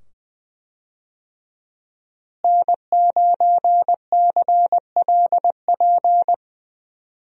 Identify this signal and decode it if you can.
Morse code, heard as a keyed tone, about 20 wpm, 700 Hz. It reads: N9CLP
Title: USA Call Signs - 5 Characters - Repeated 2x - 20wpm